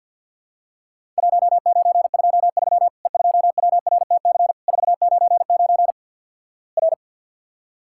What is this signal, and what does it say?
Morse code, heard as a keyed tone, about 50 wpm, 700 Hz. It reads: 1923 E2WRTC 499 R
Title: Worked All Europe DX Contest - QTC Exchange with Repeats - 50wpm